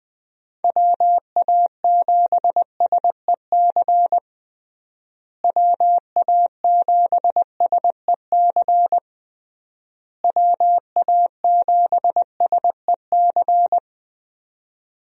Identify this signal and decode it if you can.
Morse code, heard as a keyed tone, about 20 wpm, 700 Hz. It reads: WA7SEC WA7SEC WA7SEC
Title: USA Call Signs - 6 Characters - Repeated 3x - 20wpm